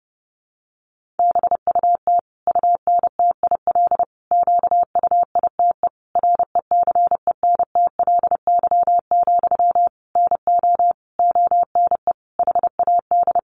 Morse code, heard as a keyed tone, about 30 wpm, 700 Hz. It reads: BUT UNTIL QUITE RECENTLY, NO ONE HAD
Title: Sentences from Top 1200 Words 30wpm